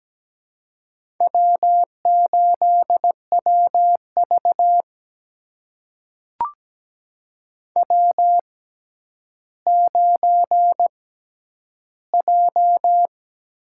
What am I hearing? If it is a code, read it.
Morse code, heard as a keyed tone, about 17 wpm, spaced as for 10 wpm, 700 Hz. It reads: W8WV W9J
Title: USA Call Signs - 4 Characters - 6x Character Spacing - 17wpm